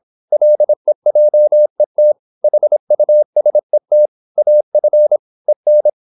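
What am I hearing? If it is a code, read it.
Morse code, heard as a keyed tone, about 26 wpm, 600 Hz. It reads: LEJET HUSET AF EN
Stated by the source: High-speed (QRQ) Morse at 26 wpm.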